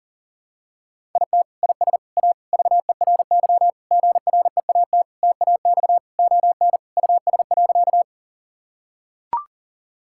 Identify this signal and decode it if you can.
Morse code, heard as a keyed tone, about 40 wpm, 700 Hz. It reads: IT IS A VERY GREAT TAX ON US.
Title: Sentences from Top 1000 Words - Review Part 5 - 40wpm